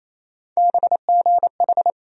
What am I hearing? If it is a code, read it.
Morse code, heard as a keyed tone, about 28 wpm, 700 Hz. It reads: BGH
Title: USA Call Sign Postfixes - 3 Letters 28wpm